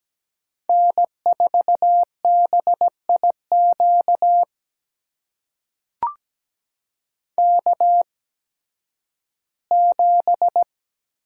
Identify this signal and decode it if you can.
Morse code, heard as a keyed tone, about 17 wpm, spaced as for 11 wpm, 700 Hz. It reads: N4BIQ K7
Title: USA Call Signs - 5 Characters - 8x Character Spacing - 17wpm